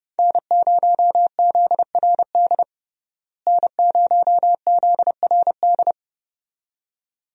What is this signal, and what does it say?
Morse code, heard as a keyed tone, about 32 wpm, 700 Hz. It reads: N0ZRD N0ZRD
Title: USA Call Signs - 5 Characters - Repeated 2x - 30wpm